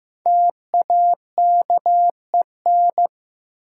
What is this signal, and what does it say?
Morse code, heard as a keyed tone, about 15 wpm, 700 Hz. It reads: TAKEN
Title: Sentences from Top 1000 Words - Review Part 4 - New Word Drill - 15wpm